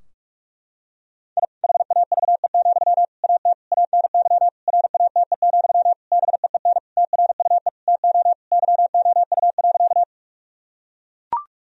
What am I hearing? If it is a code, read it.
Morse code, heard as a keyed tone, about 42 wpm, 700 Hz. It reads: I HAVE, AT ANY RATE, BEEN TRUE TO YOU.
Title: Sentences from Top 1000 Words - Review Part 4 - 45wpm